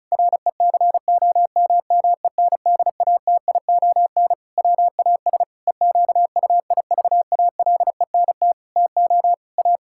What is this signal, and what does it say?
Morse code, heard as a keyed tone, about 36 wpm, 700 Hz. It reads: RECOMMENDATION WAS EQUIVALENT TO A
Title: Sentences from Top 1600 Words 35wpm